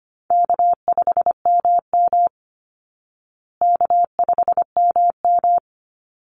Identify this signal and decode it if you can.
Morse code, heard as a keyed tone, about 25 wpm, 700 Hz. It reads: K5MM K5MM
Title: Easy Call Signs Repeated 2x - 25wpm